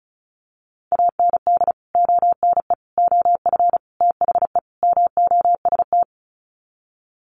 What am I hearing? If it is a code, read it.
Morse code, heard as a keyed tone, about 34 wpm, 700 Hz. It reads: AND ONE OF THE MOST
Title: Sets of 5 Words Encore 35wpm